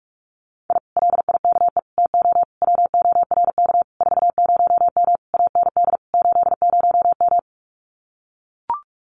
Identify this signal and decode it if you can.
Morse code, heard as a keyed tone, about 44 wpm, 700 Hz. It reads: I LIKE TO WORK 40M AND 80M
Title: Common QSO Phrases Encore 45wpm